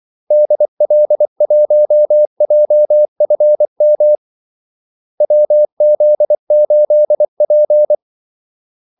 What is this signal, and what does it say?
Morse code, heard as a keyed tone, about 24 wpm, 600 Hz. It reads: DL1JFM WZ8P
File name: Callsigns-24wpm